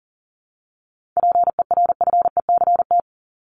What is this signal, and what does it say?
Morse code, heard as a keyed tone, about 40 wpm, 700 Hz. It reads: PERFECT
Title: Top 751-1000 Words 40wpm